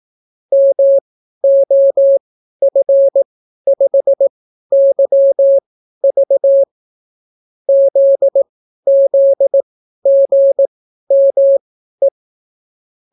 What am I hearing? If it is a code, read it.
Morse code, heard as a keyed tone, about 18 wpm, spaced as for 12 wpm, 550 Hz. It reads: MOF5YV ZZGME